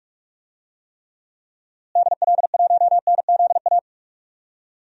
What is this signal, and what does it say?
Morse code, heard as a keyed tone, about 46 wpm, 700 Hz. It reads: DL1NZA
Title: DXCC-100 Call Signs - 6 Characters - Repeated 3x - 45wpm